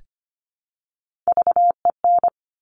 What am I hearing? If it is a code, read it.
Morse code, heard as a keyed tone, about 25 wpm, 700 Hz. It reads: VEN
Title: Top Three Letter Combinations in Words 25wpm